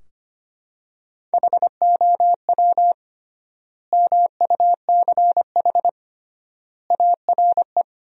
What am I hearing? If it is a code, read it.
Morse code, heard as a keyed tone, about 25 wpm, 700 Hz. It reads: HOW MUCH ARE
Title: 3x Word-Spacing - Sentences from Top 200 Words - 25wpm